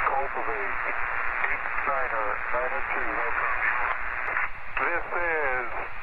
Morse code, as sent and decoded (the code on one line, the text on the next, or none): none